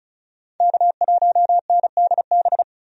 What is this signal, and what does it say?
Morse code, heard as a keyed tone, about 36 wpm, 700 Hz. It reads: K1NDB
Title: USA Call Signs - 5 Characters - 35wpm